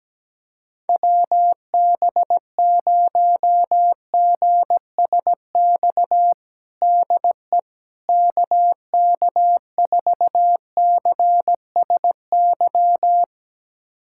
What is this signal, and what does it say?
Morse code, heard as a keyed tone, about 17 wpm, 700 Hz. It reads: WB0GSX DE KK4CSY
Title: USA Callsign DE Callsign - 6 Characters - 17wpm